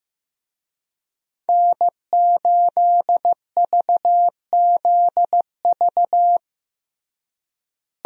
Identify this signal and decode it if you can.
Morse code, heard as a keyed tone, about 15 wpm, 700 Hz. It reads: N8VZV
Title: USA Call Signs - 5 Characters - Repeated 3x - 15wpm